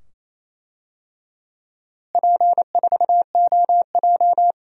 Morse code, heard as a keyed tone, about 28 wpm, 700 Hz. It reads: P4OJ
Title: DXCC-100 Call Signs - 4 Characters - Repeated 2x - 28wpm